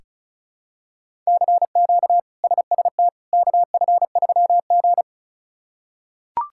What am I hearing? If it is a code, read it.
Morse code, heard as a keyed tone, about 36 wpm, 700 Hz. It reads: CQ SST KF3G
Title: K1USN SST QSO Practice - 35wpm